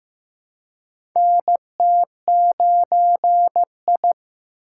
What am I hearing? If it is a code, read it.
Morse code, heard as a keyed tone, about 15 wpm, 700 Hz. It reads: NT9I
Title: USA Call Signs - 2x Character Spacing - 15wpm